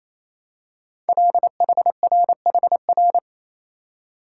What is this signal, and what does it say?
Morse code, heard as a keyed tone, about 28 wpm, 700 Hz. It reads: LHRHR
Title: Code Groups - 5 Letter:Numbers - 28wpm